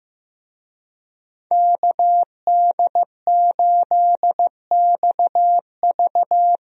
Morse code, heard as a keyed tone, about 15 wpm, 700 Hz. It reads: KD8XV
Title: USA Call Signs - 5 Characters - 15wpm